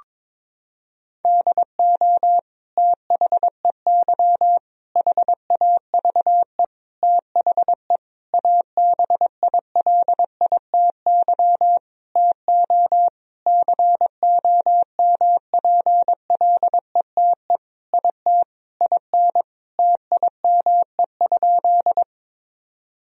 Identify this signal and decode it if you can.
Morse code, heard as a keyed tone, about 22 wpm, 700 Hz. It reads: DO THEY HAVE THE ABILITY TO COMPLETE IT IN TIME?
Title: Sentences from Top 1000 Words - Review Part 3 - 22wpm